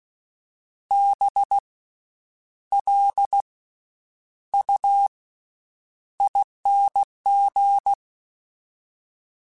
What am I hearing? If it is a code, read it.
Morse code, heard as a keyed tone, about 16 wpm, spaced as for 8 wpm, 800 Hz. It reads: BLUING
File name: Special INGs / Special INGs-mp3s / VerySlowToFaster Ing @15